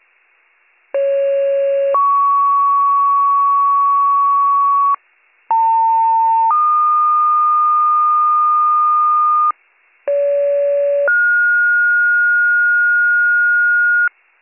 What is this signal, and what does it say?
Two-tone page, heard as a one-second tone, then a three-second tone, about 550 then 1120 Hz, about 900 then 1220 Hz, about 570 then 1430 Hz. It is heard from 0.9 to 14.1 s.